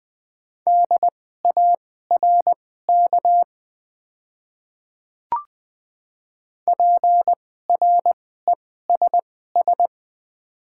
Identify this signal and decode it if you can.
Morse code, heard as a keyed tone, about 20 wpm, spaced as for 15 wpm, 690 Hz. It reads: DARK PRESS
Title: 2x Character Spacing - Top 500 Words - 20wpm